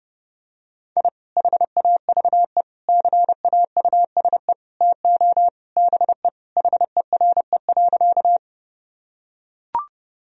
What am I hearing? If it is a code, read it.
Morse code, heard as a keyed tone, about 30 wpm, 700 Hz. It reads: I HAVE CAUSE TO BE HERE.
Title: Sentences from Top 1000 Words - Review Part 1 - 30wpm